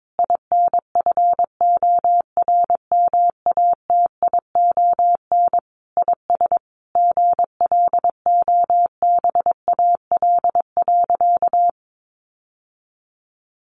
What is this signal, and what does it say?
Morse code, heard as a keyed tone, about 22 wpm, 700 Hz. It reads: INFORMATION IS GLOBAL.
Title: Sentences from Top 1200 Words 22wpm